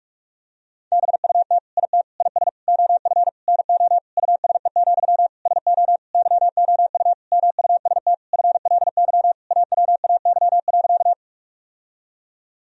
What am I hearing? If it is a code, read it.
Morse code, heard as a keyed tone, about 46 wpm, 700 Hz. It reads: BUT IT IS OF NO USE, SO YOU MUST FLY AWAY.
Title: Sentences from Top 1000 Words - Review Part 2 - 45wpm